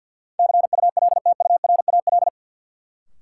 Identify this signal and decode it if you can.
Morse code, heard as a keyed tone, about 53 wpm, 700 Hz. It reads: CULTURAL
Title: Sentences from Top 1500 Words - New Word Drill - 50wpm